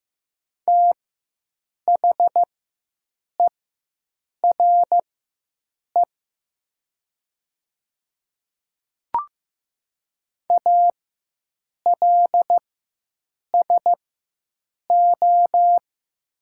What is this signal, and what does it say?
Morse code, heard as a keyed tone, about 15 wpm, spaced as for 7 wpm, 700 Hz. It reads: THERE ALSO
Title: 4x Character Spacing - Top 300 Words - 15wpm